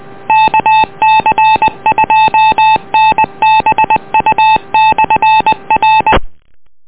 Morse code, heard as a keyed tone, about 20 wpm, 870 Hz. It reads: KC2NBU/R